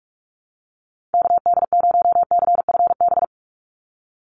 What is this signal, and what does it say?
Morse code, heard as a keyed tone, about 46 wpm, 690 Hz. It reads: KD0CFB
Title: USA Call Signs 45wpm